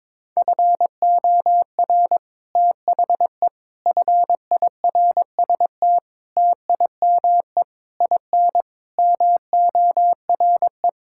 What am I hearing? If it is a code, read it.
Morse code, heard as a keyed tone, about 22 wpm, 700 Hz. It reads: FOR THE FIRST TIME IN MORE
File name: Sets of 9 Words - 22wpm